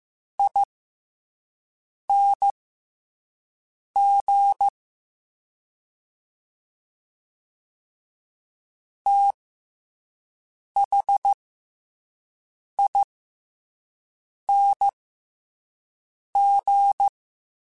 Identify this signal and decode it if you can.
Morse code, heard as a keyed tone, about 15 wpm, spaced as for 5 wpm, 800 Hz. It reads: ING THING
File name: Special INGs / Special INGs-mp3s / Slow ing @5